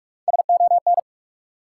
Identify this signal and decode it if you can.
Morse code, heard as a keyed tone, about 42 wpm, 700 Hz. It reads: SON
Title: Three Letter Words - 45wpm